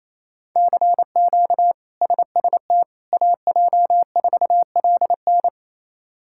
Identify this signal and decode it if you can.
Morse code, heard as a keyed tone, about 28 wpm, 700 Hz. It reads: CQ SST AJ4LN
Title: K1USN SST QSO Practice - 28wpm